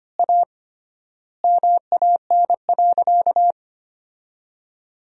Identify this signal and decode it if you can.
Morse code, heard as a keyed tone, about 25 wpm, 700 Hz. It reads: A MAN.
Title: 3x Word-Spacing - Sentences from Top 500 Words - 25wpm